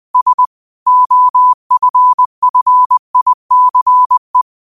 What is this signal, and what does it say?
Morse code, heard as a keyed tone, about 20 wpm, 1000 Hz. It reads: S OFFICE